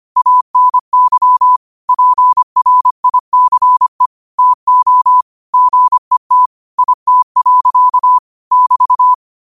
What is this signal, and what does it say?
Morse code, heard as a keyed tone, about 25 wpm, 1000 Hz. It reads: ANY PRICE TO GET IT. <BT>